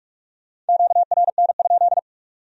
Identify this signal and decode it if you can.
Morse code, heard as a keyed tone, about 56 wpm, 700 Hz. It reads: QRN?
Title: Speed-Racing Q-Codes - 45wpm